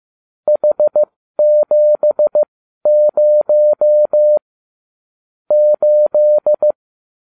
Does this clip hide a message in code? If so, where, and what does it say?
Yes: Morse code from 0.5 to 6.7 s, heard as a keyed tone, about 15 wpm, spaced as for 12 wpm, 600 Hz. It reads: H70 8